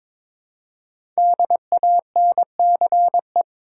Morse code, heard as a keyed tone, about 22 wpm, 700 Hz. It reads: DANCE
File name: 5 Letter Words - First 3 Letters Given - 22wpm